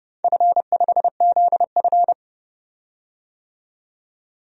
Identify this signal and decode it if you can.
Morse code, heard as a keyed tone, about 29 wpm, 700 Hz. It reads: F5ZF
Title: DXCC-100 Call Signs - 4 Characters - 30wpm